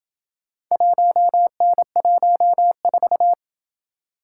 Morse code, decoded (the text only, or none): 1N14